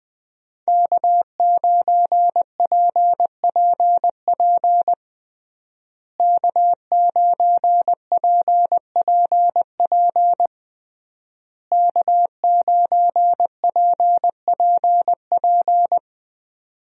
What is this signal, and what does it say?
Morse code, heard as a keyed tone, about 21 wpm, 700 Hz. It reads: K9PPP K9PPP K9PPP
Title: USA Call Signs - 5 Characters - Repeated 3x - 20wpm